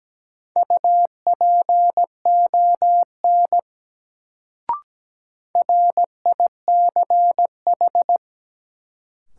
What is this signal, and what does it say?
Morse code, heard as a keyed tone, about 17 wpm, 700 Hz. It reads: UPON RICH
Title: Sentences from Top 1100 Words - New Word Drill Alt - 17wpm